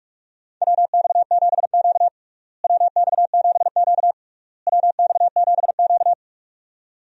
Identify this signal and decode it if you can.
Morse code, heard as a keyed tone, about 46 wpm, 700 Hz. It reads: WX7Q WX7Q WX7Q
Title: USA Call Signs - 4 Characters - Repeated 3x - 45wpm